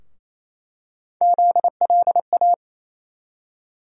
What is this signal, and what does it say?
Morse code, heard as a keyed tone, about 28 wpm, 700 Hz. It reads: ZLA
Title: USA Call Sign Postfixes - 3 Letters 28wpm